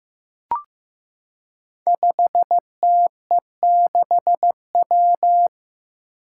Morse code, heard as a keyed tone, about 15 wpm, 700 Hz. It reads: 5TE6W
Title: Five Challenging Letter-Number Combinations 15wpm